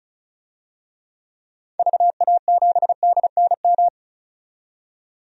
Morse code, heard as a keyed tone, about 36 wpm, 700 Hz. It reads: VA7DNM
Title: DXCC-100 Call Signs - 6 Characters - 35wpm